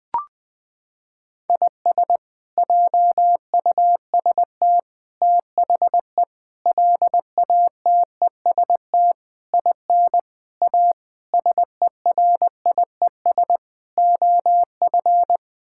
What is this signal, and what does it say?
Morse code, heard as a keyed tone, about 20 wpm, 700 Hz. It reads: IS JUST THE LATEST IN A SERIES OF
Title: Sets of 8 Words - 20wpm